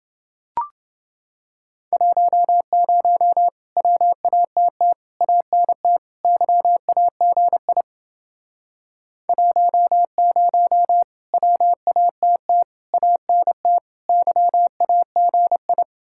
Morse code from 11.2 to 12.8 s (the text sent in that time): WATT